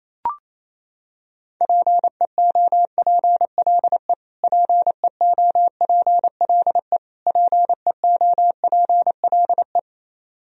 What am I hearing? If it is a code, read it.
Morse code, heard as a keyed tone, about 28 wpm, 700 Hz. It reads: PEOPLE PEOPLE PEOPLE
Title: Top 100 Words Repeated 28wpm